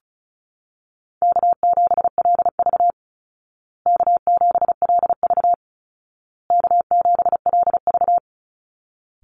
Morse code, heard as a keyed tone, about 34 wpm, 700 Hz. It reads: K7LV K7LV K7LV
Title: Call Signs Repeated 2x - 35wpm